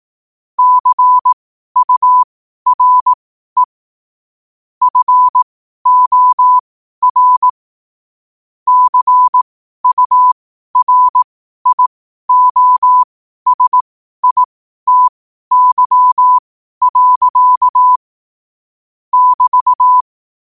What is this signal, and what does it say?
Morse code, heard as a keyed tone, about 18 wpm, spaced as for 12 wpm, 1000 Hz. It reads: CURE FOR CURIOSITY. <BT>
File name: qotdfiles / 12wpm